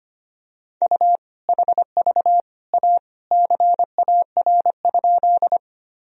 Morse code, heard as a keyed tone, about 25 wpm, 700 Hz. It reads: U HV A CAR?
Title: Sentences from Top 300 Words and Abbreviations 25wpm